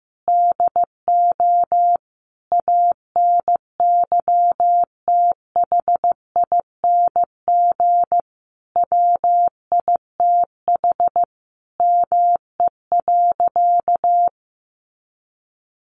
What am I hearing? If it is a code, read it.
Morse code, heard as a keyed tone, about 15 wpm, 700 Hz. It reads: DO ANYTHING WITH ME.
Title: Sentences from Top 1000 Words 15wpm